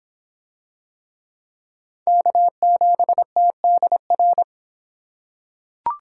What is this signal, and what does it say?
Morse code, heard as a keyed tone, about 25 wpm, 700 Hz. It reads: K7TDR